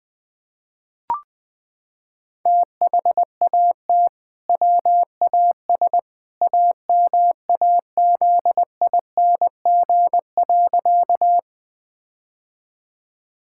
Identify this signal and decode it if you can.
Morse code, heard as a keyed tone, about 20 wpm, 700 Hz. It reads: THAT WAS AMAZING.
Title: Sentences with 2-Word Phrases - 20wpm